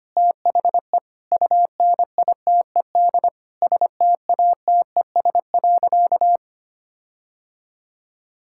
Morse code, heard as a keyed tone, about 25 wpm, 700 Hz. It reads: THE UNITED STATES.
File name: practice / 25wpm / Sentences with 4-Word Phrases - 25wpm